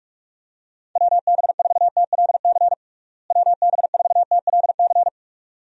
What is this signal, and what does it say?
Morse code, heard as a keyed tone, about 46 wpm, 700 Hz. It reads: WB4TLC WB4TLC
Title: USA Call Signs - 6 Characters - Repeated 2x - 45wpm